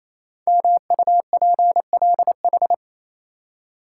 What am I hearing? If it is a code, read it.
Morse code, heard as a keyed tone, about 28 wpm, 700 Hz. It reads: MUPLH
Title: Code Groups - 5 Letters - 28wpm